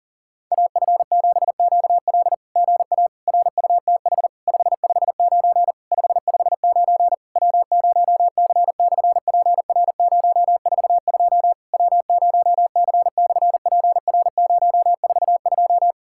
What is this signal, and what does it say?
Morse code, heard as a keyed tone, about 40 wpm, 700 Hz. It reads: AF7QL GA RUTH 559 559 W0C/PR042 W0C/PR042
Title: SOTA QSO Practice 40wpm